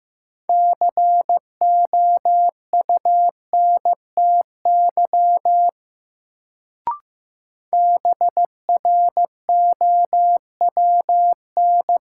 Morse code, heard as a keyed tone, about 15 wpm, 700 Hz. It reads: COUNTY BROWN
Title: Sentences from Top 1200 Words - New Word Drill - 15wpm